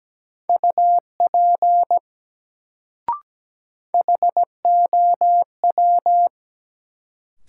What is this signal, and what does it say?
Morse code, heard as a keyed tone, about 17 wpm, 700 Hz. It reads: UP HOW
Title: Sentences from Top 100 Words - New Word Drill - 17wpm